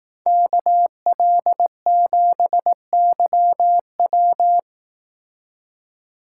Morse code, heard as a keyed tone, about 18 wpm, 700 Hz. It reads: KL7YW
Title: Speed Racing - USA Call Signs 15wpm